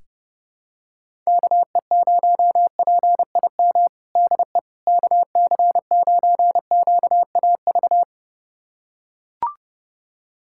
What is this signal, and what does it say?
Morse code, heard as a keyed tone, about 30 wpm, 690 Hz. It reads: KE0PIM DE KC9QAV
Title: USA Callsign DE Callsign - 6 Characters - 30wpm